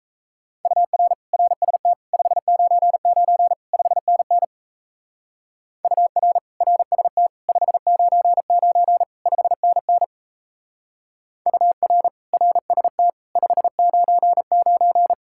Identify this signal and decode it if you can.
Morse code, heard as a keyed tone, about 35 wpm, 700 Hz. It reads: UR RST 599 5NN UR RST 599 5NN UR RST 599
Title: Speed-Racing Common QSO Phrases - 28wpm